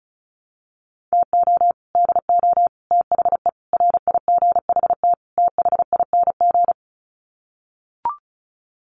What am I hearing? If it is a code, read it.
Morse code, heard as a keyed tone, about 34 wpm, 700 Hz. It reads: TO DO THE RIGHT THING
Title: Sets of 5 Words 35wpm